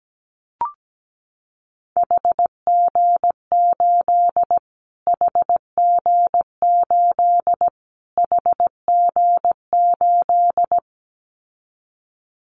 Morse code, heard as a keyed tone, about 17 wpm, 700 Hz. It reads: HG8 HG8 HG8
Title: Callsign Prefixes for Top 25 DXCC - Repeated 3x - 17wpm